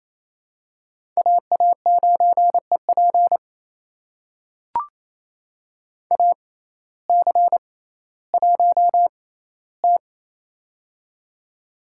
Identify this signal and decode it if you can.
Morse code, heard as a keyed tone, about 28 wpm, spaced as for 15 wpm, 700 Hz. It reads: AA9EP AC1T
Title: USA Call Signs - 6x Character Spacing - 28wpm